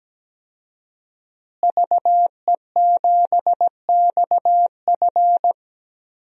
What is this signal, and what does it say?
Morse code, heard as a keyed tone, about 17 wpm, 700 Hz. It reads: VE7XF
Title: DXCC-100 Call Signs - 5 Characters - Repeated 2x - 17wpm